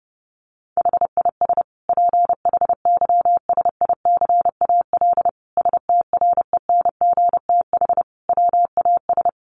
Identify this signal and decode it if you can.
Morse code, heard as a keyed tone, about 30 wpm, 700 Hz. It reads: HIS PHYSICAL STRENGTH WAS
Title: Sentences from Top 1400 Words 30wpm